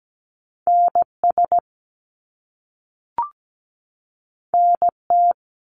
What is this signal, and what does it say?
Morse code, heard as a keyed tone, about 17 wpm, 700 Hz. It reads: NS NT
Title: Canadian Provinces and Territories 17wpm